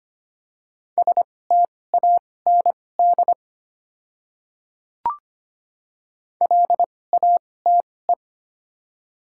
Morse code, heard as a keyed tone, about 25 wpm, spaced as for 18 wpm, 700 Hz. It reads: STAND LATE
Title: 2x Character Spacing - Top 300 Words - 25wpm